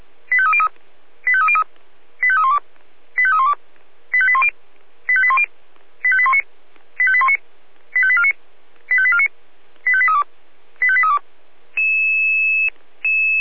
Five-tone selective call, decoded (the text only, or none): 85303 85303 85312 85312 85810 85810 85819 85819 85849 85849 85832 85832